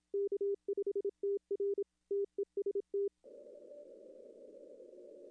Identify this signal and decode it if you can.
Morse code, heard as a keyed tone, about 26 wpm, 410 Hz. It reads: K5TR TEST
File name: contests / arrl-160-k5tr-op-w5tn -0400